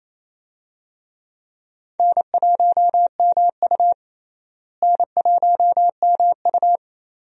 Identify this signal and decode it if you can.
Morse code, heard as a keyed tone, about 28 wpm, 700 Hz. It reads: N1MU N1MU
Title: USA Call Signs - 4 Characters - Repeated 2x - 28wpm